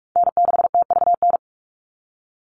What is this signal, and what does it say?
Morse code, heard as a keyed tone, about 44 wpm, 700 Hz. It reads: N6TVN
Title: Call Signs - 45wpm